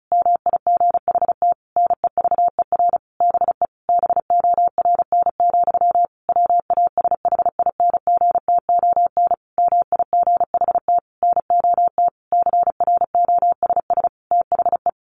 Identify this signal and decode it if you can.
Morse code, heard as a keyed tone, about 34 wpm, 700 Hz. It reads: MIGHT NEVER BE BORN, WASHINGTON MIGHT NOT CROSS THE
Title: A Sound Of Thunder 35wpm